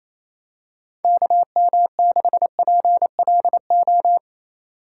Morse code, heard as a keyed tone, about 27 wpm, 700 Hz. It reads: KM6PLO
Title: USA Call Signs - 4x Character Spacing - 28wpm